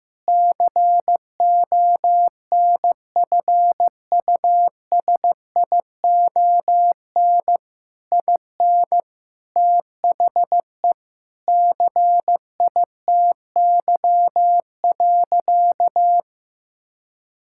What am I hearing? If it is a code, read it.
Morse code, heard as a keyed tone, about 15 wpm, 700 Hz. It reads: CONFUSION IN THE CITY.
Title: Sentences from Top 1600 Words 15wpm